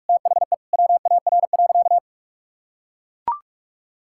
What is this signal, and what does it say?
Morse code, heard as a keyed tone, about 41 wpm, 700 Hz. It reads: THE WAR.
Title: Sentences from Top 1000 Words - Review Part 6 - 45wpm